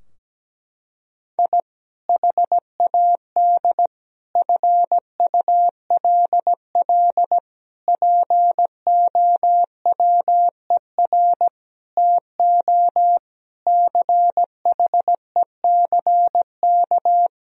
Morse code, heard as a keyed tone, about 17 wpm, 700 Hz. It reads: I HAD FULL POWER TO CHECK
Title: Sentences from Top 1000 Words - Review Part 3 - 17wpm